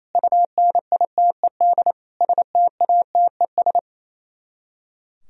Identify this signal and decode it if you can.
Morse code, heard as a keyed tone, about 33 wpm, 700 Hz. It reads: UNITED STATES
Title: Speed-Racing - Sets of 5 Words 28wpm